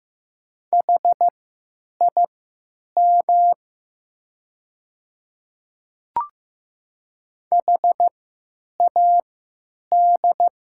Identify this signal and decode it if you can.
Morse code, heard as a keyed tone, about 15 wpm, spaced as for 9 wpm, 700 Hz. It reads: HIM HAD
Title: 3x Character Spacing - Top 100 Words - 15wpm